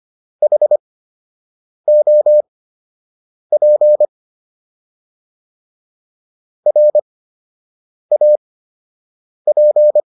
Morse code, heard as a keyed tone, about 25 wpm, spaced as for 7 wpm, 600 Hz. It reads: HOP RAP